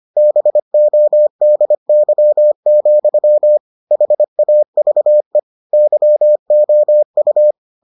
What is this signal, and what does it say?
Morse code, heard as a keyed tone, about 25 wpm, 600 Hz. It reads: BODY, HAVE YOU